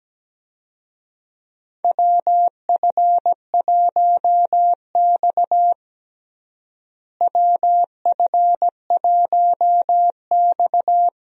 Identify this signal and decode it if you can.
Morse code, heard as a keyed tone, about 17 wpm, 700 Hz. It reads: WF1X WF1X
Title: USA Call Signs - 4 Characters - Repeated 2x - 17wpm